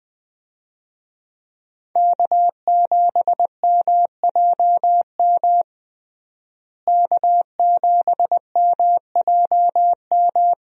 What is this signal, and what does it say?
Morse code, heard as a keyed tone, about 20 wpm, 700 Hz. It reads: K7MJM K7MJM
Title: USA Call Signs - 5 Characters - Repeated 3x - 20wpm